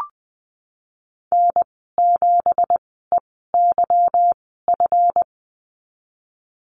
Morse code, heard as a keyed tone, about 20 wpm, spaced as for 15 wpm, 700 Hz. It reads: N7EYF
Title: 173 Lesson 28g - Number 9 - Call Signs 20wpm